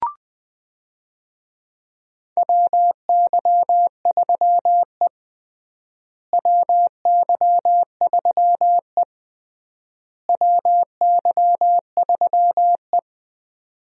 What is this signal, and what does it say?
Morse code, heard as a keyed tone, about 20 wpm, 700 Hz. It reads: WY3E WY3E WY3E
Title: USA Call Signs - 4 Characters - Repeated 3x - 20wpm